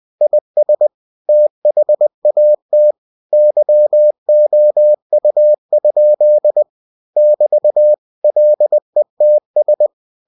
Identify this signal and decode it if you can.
Morse code, heard as a keyed tone, about 20 wpm, 600 Hz. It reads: IS THAT YOU? <BT> LETS